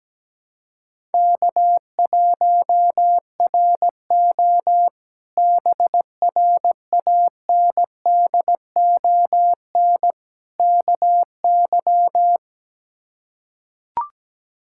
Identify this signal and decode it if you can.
Morse code, heard as a keyed tone, about 17 wpm, 700 Hz. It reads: K1RO BRANDON KY
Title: K1USN SST QSO Practice - 17wpm